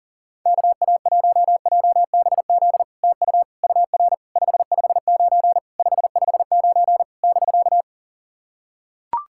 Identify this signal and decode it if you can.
Morse code, heard as a keyed tone, about 40 wpm, 700 Hz. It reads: KA1JBZ TU UR 559 559 <BK>
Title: SOTA QSO Practice 40wpm